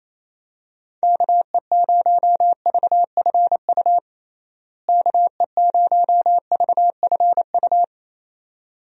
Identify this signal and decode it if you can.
Morse code, heard as a keyed tone, about 28 wpm, 700 Hz. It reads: KE0VFU KE0VFU
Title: USA Call Signs - 6 Characters - Repeated 2x - 28wpm